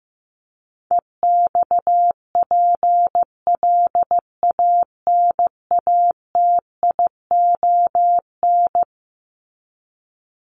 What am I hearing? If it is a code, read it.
Morse code, heard as a keyed tone, about 15 wpm, 700 Hz. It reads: EXPLANATION
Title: Top 1751-2000 Words 15wpm